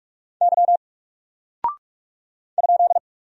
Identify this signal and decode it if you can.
Morse code, heard as a keyed tone, about 44 wpm, 700 Hz. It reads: Y ?
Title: Single Character (Letter:Number:Punctuation:Prosign) Alt 45wpm